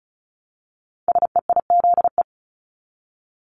Morse code, heard as a keyed tone, about 36 wpm, 700 Hz. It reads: SEIZE
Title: Multi-letter Phonemes Repeated 3x 35wpm